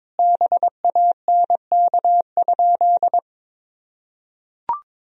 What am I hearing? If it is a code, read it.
Morse code, heard as a keyed tone, about 22 wpm, 700 Hz. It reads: BANK?
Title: Sentences from Top 1000 Words - Review Part 4 - 22wpm